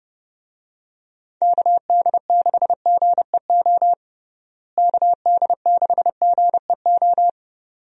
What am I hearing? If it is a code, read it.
Morse code, heard as a keyed tone, about 30 wpm, 700 Hz. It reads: KD6GEO KD6GEO
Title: USA Call Signs - 6 Characters - Repeated 2x - 30wpm